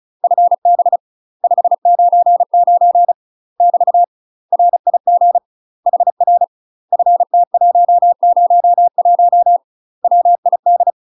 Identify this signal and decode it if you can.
Morse code, heard as a keyed tone, about 36 wpm, 700 Hz. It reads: FB 599 <BT> RIG HR FT101 WID
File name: UK MASTER QSOs TRAINING/35WPM